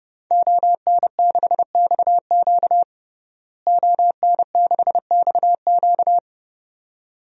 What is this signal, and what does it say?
Morse code, heard as a keyed tone, about 32 wpm, 700 Hz. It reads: ON6XQ ON6XQ
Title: DXCC-100 Call Signs - 5 Characters - Repeated 2x - 30wpm